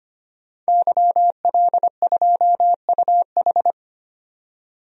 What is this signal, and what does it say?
Morse code, heard as a keyed tone, about 25 wpm, 690 Hz. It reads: YL2UH
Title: DXCC-100 Call Signs - 5 Characters - 25wpm